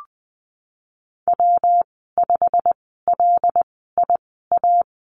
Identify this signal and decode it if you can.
Morse code, heard as a keyed tone, about 20 wpm, spaced as for 15 wpm, 700 Hz. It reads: W5LIA
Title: 128 Lesson 21g - Number 6 - Call Signs 20wpm